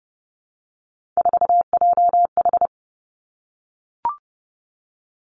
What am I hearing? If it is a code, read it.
Morse code, heard as a keyed tone, about 30 wpm, 700 Hz. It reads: 4JH
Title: Three Challenging Letter-Number Combinations 30wpm